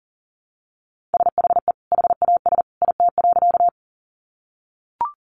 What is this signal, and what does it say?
Morse code, heard as a keyed tone, about 40 wpm, 700 Hz. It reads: SHE HAS IT.
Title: Sentences from Top 100 Words 40wpm